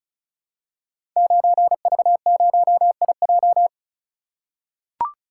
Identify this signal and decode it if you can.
Morse code, heard as a keyed tone, about 36 wpm, 700 Hz. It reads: 9V0IJ
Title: Code Groups - 5 Letter:Numbers - 35wpm